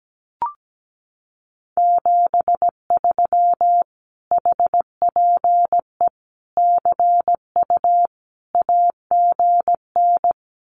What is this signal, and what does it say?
Morse code, heard as a keyed tone, about 17 wpm, 700 Hz. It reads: 73 HPE CU AGN
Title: Common QSO Phrases Encore 17wpm